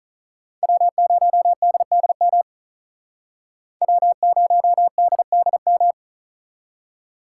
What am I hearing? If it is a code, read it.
Morse code, heard as a keyed tone, about 43 wpm, 700 Hz. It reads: W0DDM W0DDM
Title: Speed Racing - USA 1x3 Call Signs 35wpm